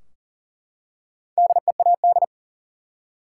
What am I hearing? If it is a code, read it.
Morse code, heard as a keyed tone, about 40 wpm, 700 Hz. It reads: DEAD
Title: Four Letter Words - 40wpm